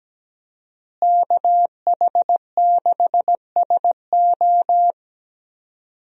Morse code, heard as a keyed tone, about 17 wpm, 700 Hz. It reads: KH6SO
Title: USA Call Signs - 5 Characters - Repeated 3x - 17wpm